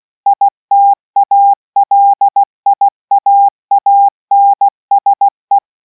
Morse code, heard as a keyed tone, about 16 wpm, 800 Hz. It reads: ITALIAANSE